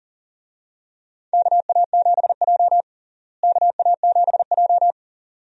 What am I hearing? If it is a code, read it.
Morse code, heard as a keyed tone, about 40 wpm, 700 Hz. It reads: KA7J KA7J
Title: USA Call Signs - 4 Characters - Repeated 2x - 40wpm